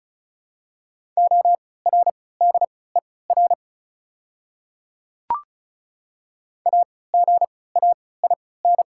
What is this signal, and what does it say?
Morse code, heard as a keyed tone, about 34 wpm, spaced as for 20 wpm, 690 Hz. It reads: ORDER AGAIN
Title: 3x Character Spacing - Top 500 Words - 35wpm